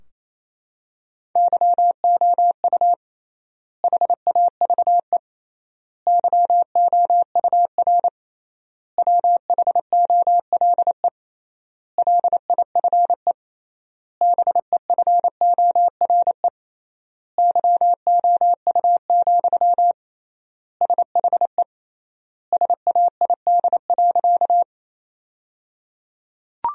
Morse code, heard as a keyed tone, about 28 wpm, 700 Hz. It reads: YOU HAVE YOUR WHOLE LIFE BEFORE YOU, SHE SAID.
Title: 3x Word-Spacing - Sentences from Top 400 Words - 28wpm